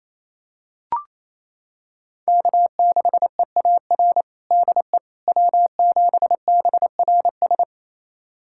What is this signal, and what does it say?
Morse code, heard as a keyed tone, about 28 wpm, 700 Hz. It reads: K6EAR DE W7BRS
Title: USA Callsign DE Callsign - 5 Characters - 28wpm